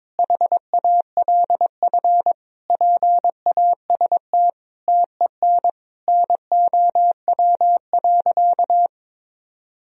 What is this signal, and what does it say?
Morse code, heard as a keyed tone, about 22 wpm, 700 Hz. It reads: HALF PAST TEN NOW.
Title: Sentences from Top 1000 Words - Review Part 3 - 22wpm